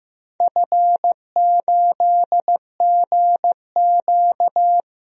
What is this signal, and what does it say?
Morse code, heard as a keyed tone, about 15 wpm, 700 Hz. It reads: F8GQ
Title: DXCC-100 Call Signs - 4 Characters - 15wpm